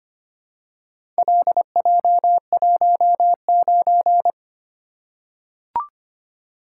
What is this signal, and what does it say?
Morse code, heard as a keyed tone, about 26 wpm, 700 Hz. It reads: LJ19
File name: practice / 25wpm / Four Letter-Number Combinations - 25wpm